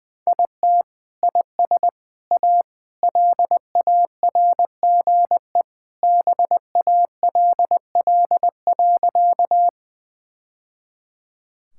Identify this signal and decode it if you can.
Morse code, heard as a keyed tone, about 21 wpm, 700 Hz. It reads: IT IS A LARGE BALL.
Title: Sentences from Top 1000 Words - Review Part 6 - 20wpm